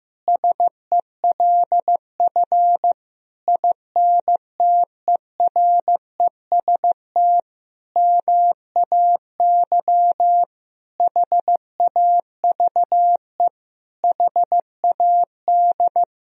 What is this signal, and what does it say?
Morse code, heard as a keyed tone, about 15 wpm, 700 Hz. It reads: SELF INTEREST MAY HAVE HAD
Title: Sentences from Top 1000 Words - Review Part 2 - 15wpm